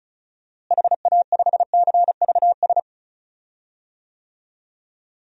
Morse code, heard as a keyed tone, about 36 wpm, 700 Hz. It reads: HA5CVS
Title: DXCC-100 Call Signs - 6 Characters - Repeated 2x - 35wpm